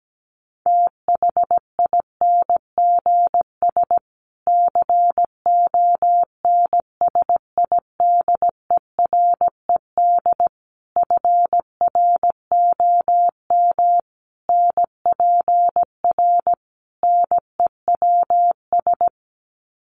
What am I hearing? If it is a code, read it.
Morse code, heard as a keyed tone, about 17 wpm, 700 Hz. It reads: THINGS CONSIDERED FROM NPR NEWS
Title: Sets of 5 Words Encore 17wpm